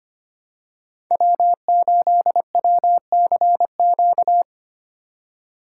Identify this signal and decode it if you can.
Morse code, heard as a keyed tone, about 25 wpm, 700 Hz. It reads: W8WCQ
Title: USA Call Signs - 5 Characters - 4x Character Spacing - 25wpm